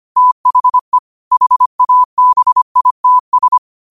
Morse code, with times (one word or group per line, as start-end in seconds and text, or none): 0.2-1.0 THE
1.3-3.6 HABITS